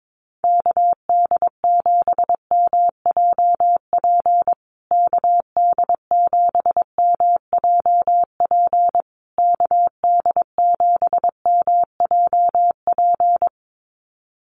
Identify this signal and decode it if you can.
Morse code, heard as a keyed tone, about 22 wpm, 690 Hz. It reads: KD7MJP KD7MJP KD7MJP
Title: USA Call Signs Repeated 3x 22wpm